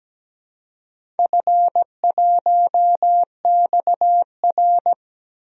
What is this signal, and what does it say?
Morse code, heard as a keyed tone, about 17 wpm, 700 Hz. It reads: F1XR
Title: DXCC-100 Call Signs - 4 Characters - 17wpm